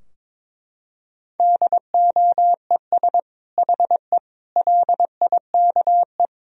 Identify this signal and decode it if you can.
Morse code, heard as a keyed tone, about 22 wpm, 700 Hz. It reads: DOES HE LIKE
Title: Sentences from Top 1000 Words - Review Part 4 - 22wpm